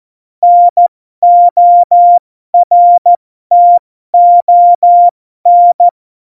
Morse code, heard as a keyed tone, about 14 wpm, spaced as for 12 wpm, 700 Hz. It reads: NORTON
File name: Morsekurs / Witze